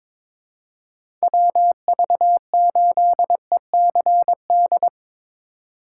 Morse code, heard as a keyed tone, about 22 wpm, 700 Hz. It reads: WV8ECD
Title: USA Call Signs - 6 Characters - 6x Character Spacing - 22wpm